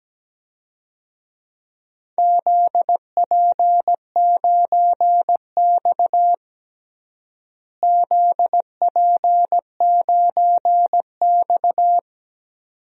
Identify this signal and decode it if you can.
Morse code, heard as a keyed tone, about 17 wpm, 700 Hz. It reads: ZP9X ZP9X
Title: DXCC-100 Call Signs - 4 Characters - Repeated 2x - 17wpm